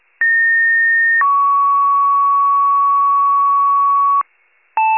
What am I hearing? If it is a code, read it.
Two-tone page, heard as a one-second tone, then a three-second tone, about 1850 then 1120 Hz. It is heard from 0.2 to 4.2 s.